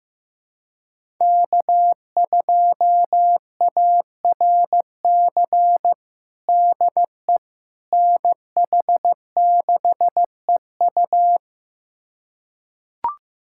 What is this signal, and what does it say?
Morse code, heard as a keyed tone, about 15 wpm, 700 Hz. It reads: K2ARC DE NH6EU
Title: USA Callsign DE Callsign - 5 Characters - 15wpm